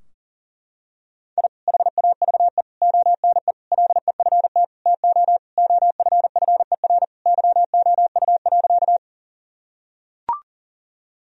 Morse code, heard as a keyed tone, about 40 wpm, 700 Hz. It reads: I HAVE ONE LEFT TO OFFER YOU.
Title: Sentences from Top 1000 Words - Review Part 4 - 40wpm